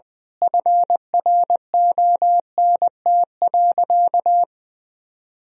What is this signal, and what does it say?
Morse code, heard as a keyed tone, about 20 wpm, 700 Hz. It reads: FRONT.
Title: Sentences from Top 1000 Words - Review Part 2 - 20wpm